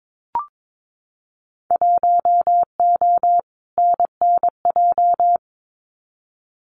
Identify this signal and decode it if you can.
Morse code, heard as a keyed tone, about 22 wpm, 700 Hz. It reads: 1O NNJ
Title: Winter Field Day Exchanges 22wpm